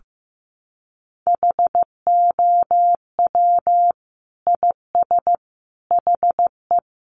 Morse code, heard as a keyed tone, about 15 wpm, 700 Hz. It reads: HOW IS HE
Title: Sentences from Top 100 Words 15wpm